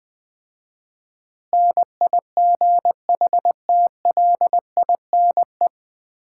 Morse code, heard as a keyed tone, about 20 wpm, 700 Hz. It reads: NIGHTLINE
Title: 4x Word-Spacing - Sets of 5 Words - 20wpm